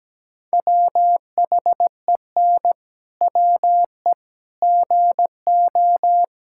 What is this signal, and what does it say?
Morse code, heard as a keyed tone, about 17 wpm, 700 Hz. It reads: WHEN WE GO
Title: Pangram Challenge v2 - 17wpm